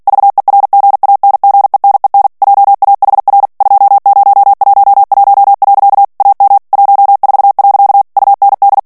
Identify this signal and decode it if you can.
Morse code, heard as a keyed tone, about 48 wpm, 800 Hz. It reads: VERGANGENEN JAHR 2011. AM 14. UND
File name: morse mp3 / ALLEMAND texte clair 2 vitesse 45 mots minute